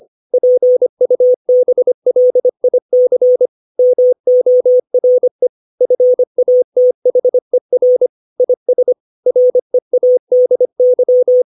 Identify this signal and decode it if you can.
Morse code, heard as a keyed tone, about 25 wpm, 500 Hz. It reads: PUBLIC MORE FATHER IS READY